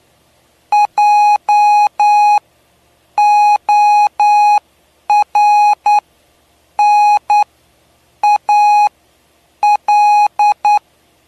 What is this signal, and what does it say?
Morse code, heard as a keyed tone, about 9 wpm, spaced as for 7 wpm, 830 Hz. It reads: JORNAL
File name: cw / py2lcd / 07-PY2LCD-Li��o 7